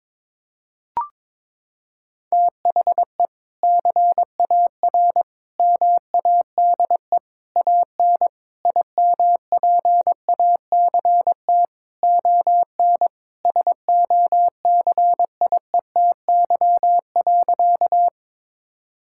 Morse code, heard as a keyed tone, about 22 wpm, 700 Hz. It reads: THE CAR MADE AN IMPACT ON SOCIETY.
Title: Sentences from Top 1000 Words - Review Part 6 - 22wpm